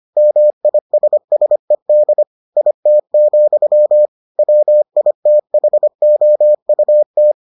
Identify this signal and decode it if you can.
Morse code, heard as a keyed tone, about 25 wpm, 600 Hz. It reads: MISSED IT, WITHOUT